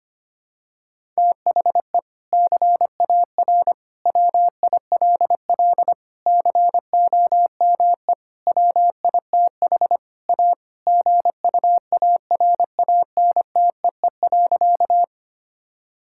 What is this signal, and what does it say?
Morse code, heard as a keyed tone, about 25 wpm, 700 Hz. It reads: THE CAR WILL COME WITH A GUARANTEE.
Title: Sentences from Top 1600 Words 25wpm